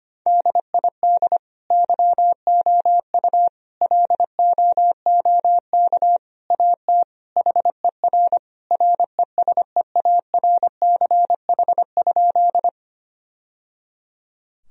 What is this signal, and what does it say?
Morse code, heard as a keyed tone, about 25 wpm, 700 Hz. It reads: DID YOU LOOK AT HER RESEARCH?
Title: Sentences from Top 1000 Words - Review Part 4 - 25wpm